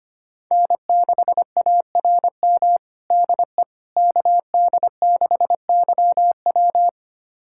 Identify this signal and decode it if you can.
Morse code, heard as a keyed tone, about 25 wpm, 700 Hz. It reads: N6ARM DE KD6YW
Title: USA Callsign DE Callsign - 5 Characters - 25wpm